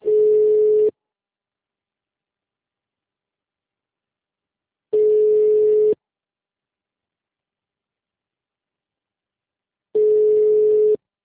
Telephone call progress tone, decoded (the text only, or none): ringback tone